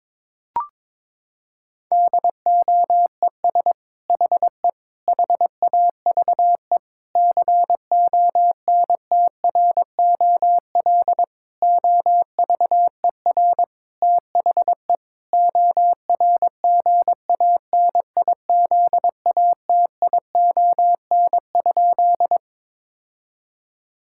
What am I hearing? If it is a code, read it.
Morse code, heard as a keyed tone, about 22 wpm, 700 Hz. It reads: DOES HE HAVE CONTROL OVER THE ORGANIZATION?
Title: Sentences from Top 1000 Words - Review Part 5 - 22wpm